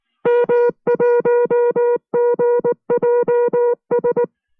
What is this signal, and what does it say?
Morse code, heard as a keyed tone, about 19 wpm, 480 Hz. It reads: M1GJS